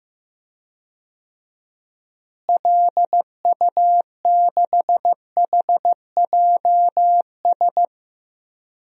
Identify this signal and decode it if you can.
Morse code, heard as a keyed tone, about 15 wpm, 700 Hz. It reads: LU6HJS
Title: DXCC-100 Call Signs - 6 Characters - Repeated 3x - 15wpm